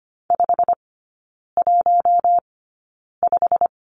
Morse code, heard as a keyed tone, about 25 wpm, spaced as for 9 wpm, 690 Hz. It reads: 515
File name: Lesson 15e - Number 5 - Three Characters 25wpm